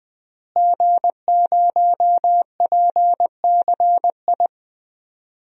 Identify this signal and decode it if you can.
Morse code, heard as a keyed tone, about 20 wpm, 700 Hz. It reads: G0PCI
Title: DXCC-100 Call Signs - 5 Characters - Repeated 3x - 20wpm